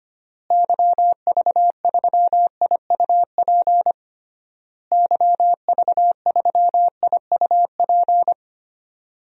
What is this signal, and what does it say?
Morse code, heard as a keyed tone, about 25 wpm, 700 Hz. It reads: YV3IUP YV3IUP
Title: DXCC-100 Call Signs - 6 Characters - Repeated 3x - 25wpm